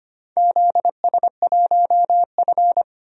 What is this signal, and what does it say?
Morse code, heard as a keyed tone, about 25 wpm, 700 Hz. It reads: ZS1F
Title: DXCC-100 Call Signs - 4 Characters - 25wpm